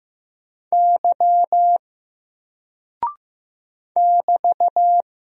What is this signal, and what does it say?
Morse code, heard as a keyed tone, about 15 wpm, 700 Hz. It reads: Y <BT>
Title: Single Character (Letter:Number:Punctuation:Prosign) 15wpm